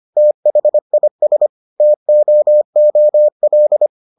Morse code, heard as a keyed tone, about 25 wpm, 600 Hz. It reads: THIS TOOL